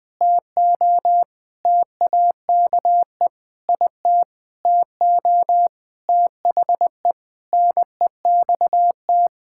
Morse code, heard as a keyed tone, about 20 wpm, 700 Hz. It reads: TO TAKE IT TO THE NEXT
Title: Sentences with 7-Word Phrases - Spoken Beginning - 20wpm